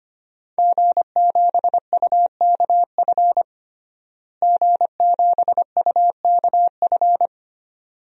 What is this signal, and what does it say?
Morse code, heard as a keyed tone, about 25 wpm, 700 Hz. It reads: G7UKF G7UKF
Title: DXCC-100 Call Signs - 5 Characters - Repeated 2x - 25wpm